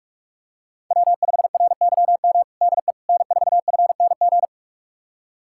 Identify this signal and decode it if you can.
Morse code, heard as a keyed tone, about 48 wpm, 700 Hz. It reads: W5RYM DE N4FNG
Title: USA Callsign DE Callsign - 5 Characters - 45wpm